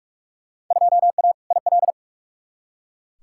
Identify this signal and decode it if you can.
Morse code, heard as a keyed tone, about 47 wpm, 700 Hz. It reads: 2A IL
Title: Field Day Exchanges - 45wpm